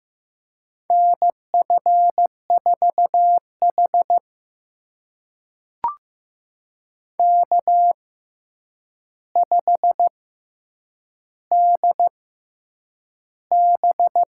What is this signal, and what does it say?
Morse code, heard as a keyed tone, about 15 wpm, spaced as for 8 wpm, 700 Hz. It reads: NF4H K5DB
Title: USA Call Signs - 4 Characters - 6x Character Spacing - 15wpm